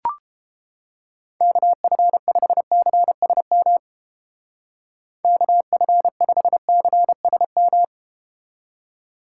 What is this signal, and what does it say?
Morse code, heard as a keyed tone, about 32 wpm, 700 Hz. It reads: KF5CSM KF5CSM